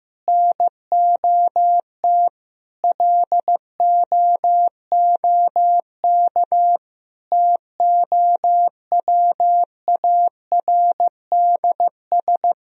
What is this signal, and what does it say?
Morse code, heard as a keyed tone, about 15 wpm, 700 Hz. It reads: NOT LOOK TOWARDS
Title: Sentences from Top 1000 Words - Review Part 2 - 15wpm